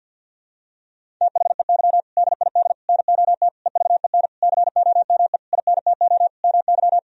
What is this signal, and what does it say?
Morse code, heard as a keyed tone, about 49 wpm, 700 Hz. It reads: THEY DID NOT EVEN COME INTO MY
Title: Sentences from Top 1000 Words - Review Part 1 - 50wpm